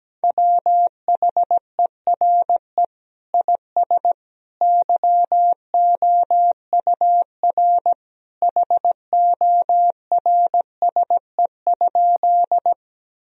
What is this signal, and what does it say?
Morse code, heard as a keyed tone, about 17 wpm, 700 Hz. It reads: WHERE IS YOUR HORSE?
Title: Sentences from Top 1000 Words - Review Part 1 - 17wpm